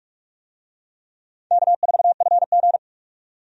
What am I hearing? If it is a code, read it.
Morse code, heard as a keyed tone, about 43 wpm, 700 Hz. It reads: K4FG
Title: USA Call Signs - 4 Characters - 45wpm